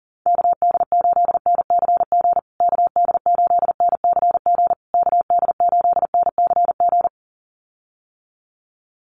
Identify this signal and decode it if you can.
Morse code, heard as a keyed tone, about 40 wpm, 700 Hz. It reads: KD8NCG KD8NCG KD8NCG
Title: USA Call Signs Repeated 3x 40wpm